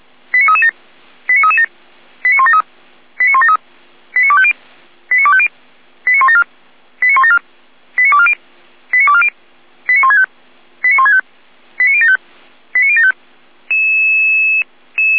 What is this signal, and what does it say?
Five-tone selective call (ZVEI), heard as quick runs of five tones, tones of about 70 ms. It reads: 89337 89337 89173 89173 89250 89250 89174 89174 89240 89240 89165 89165 89075 89075